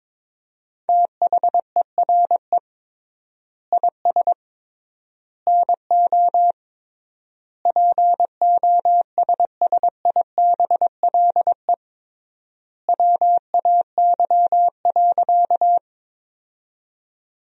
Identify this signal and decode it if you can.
Morse code, heard as a keyed tone, about 22 wpm, 700 Hz. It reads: THERE IS NO POSSIBLE WAY.
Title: 3x Word-Spacing - Sentences from Top 500 Words - 22wpm